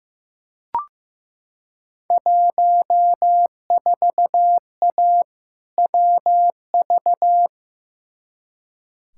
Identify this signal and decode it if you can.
Morse code, heard as a keyed tone, about 15 wpm, 700 Hz. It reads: 14A WV
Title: Field Day Exchanges Encore - 15wpm